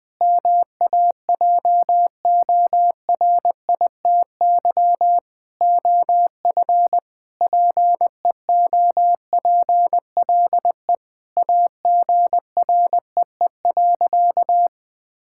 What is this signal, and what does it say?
Morse code, heard as a keyed tone, about 20 wpm, 700 Hz. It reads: MAJORITY OF PEOPLE AGREE.
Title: Sentences with 4-Word Phrases - 20wpm